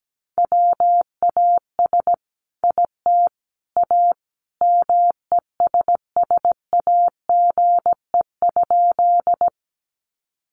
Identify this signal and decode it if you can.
Morse code, heard as a keyed tone, about 17 wpm, 700 Hz. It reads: WAS IT A MESSAGE?
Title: Sentences from Top 1100 Words 17wpm